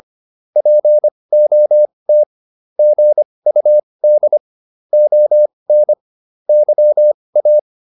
Morse code, heard as a keyed tone, about 25 wpm, spaced as for 20 wpm, 600 Hz. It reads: POT GUD ON YA